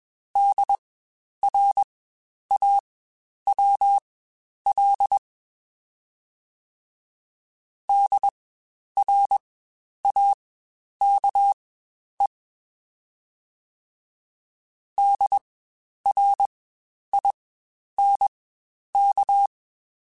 Morse code, heard as a keyed tone, about 21 wpm, spaced as for 10 wpm, 800 Hz. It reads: DRAWL DRAKE DRINK
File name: Spaced Alphas / Spaced Alphas mp3s / Ds sp @20